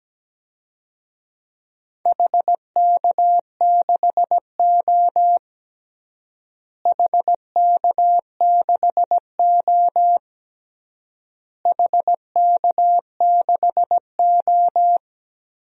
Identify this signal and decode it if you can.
Morse code, heard as a keyed tone, about 17 wpm, 700 Hz. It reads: HK6O HK6O HK6O
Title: DXCC-100 Call Signs - 4 Characters - Repeated 3x - 17wpm